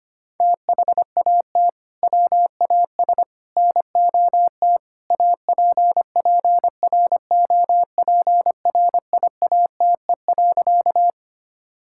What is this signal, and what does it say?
Morse code, heard as a keyed tone, about 25 wpm, 700 Hz. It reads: THAT WAS NOT APPROPRIATE.
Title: Sentences from Top 1000 Words - Review Part 5 - 25wpm